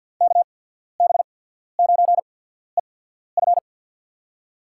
Morse code, heard as a keyed tone, about 50 wpm, spaced as for 14 wpm, 700 Hz. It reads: KB9EF
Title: USA Call Signs - 5 Characters - 8x Character Spacing - 50wpm